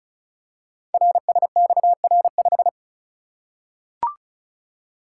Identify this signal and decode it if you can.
Morse code, heard as a keyed tone, about 35 wpm, 700 Hz. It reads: RSXR5
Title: Code Groups - 5 Letter:Numbers - 35wpm